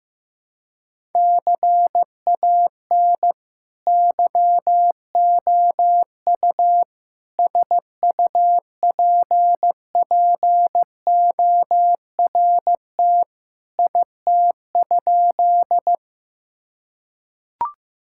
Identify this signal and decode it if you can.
Morse code, heard as a keyed tone, about 15 wpm, 700 Hz. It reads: CAN YOU SUPPORT IT?
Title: Sentences from Top 1000 Words - Review Part 5 - 15wpm